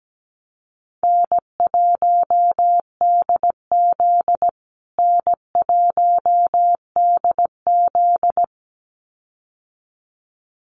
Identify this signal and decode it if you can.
Morse code, heard as a keyed tone, about 17 wpm, 700 Hz. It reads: N1DZ N1DZ
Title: USA Call Signs Repeated 2x 17wpm